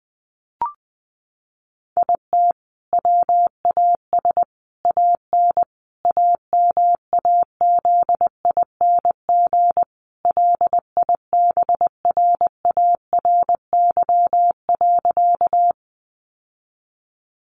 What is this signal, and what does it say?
Morse code, heard as a keyed tone, about 20 wpm, 700 Hz. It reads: IT WAS AN AMAZING LIBRARY.
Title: Sentences from Top 1500 Words 20wpm